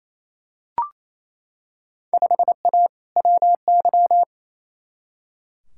Morse code, heard as a keyed tone, about 28 wpm, 700 Hz. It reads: 5A WY
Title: Field Day Exchanges Encore - 28wpm